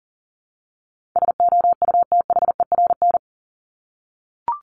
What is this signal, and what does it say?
Morse code, heard as a keyed tone, about 40 wpm, 700 Hz. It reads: SOUTHERN
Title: Top 1751-2000 Words 40wpm